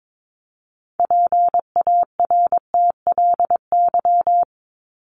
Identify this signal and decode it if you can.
Morse code, heard as a keyed tone, about 22 wpm, 700 Hz. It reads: PARTLY
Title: LY Suffix - Top 100 Words with Base Words 22wpm